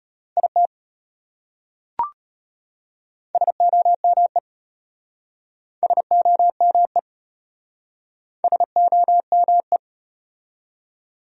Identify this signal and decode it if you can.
Morse code, heard as a keyed tone, about 31 wpm, 700 Hz. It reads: IT SOME SOME SOME
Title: Speed-Racing Top 100 Words 25wpm